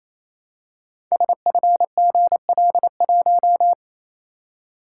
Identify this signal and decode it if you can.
Morse code, heard as a keyed tone, about 27 wpm, 700 Hz. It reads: SFGL1
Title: Code Groups - 5 Letter:Numbers - 28wpm